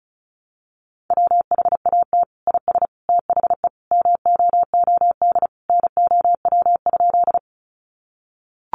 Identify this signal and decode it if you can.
Morse code, heard as a keyed tone, about 34 wpm, 700 Hz. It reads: WHAT IS THE MOOD NOW?
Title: Sentences from Top 1100 Words 35wpm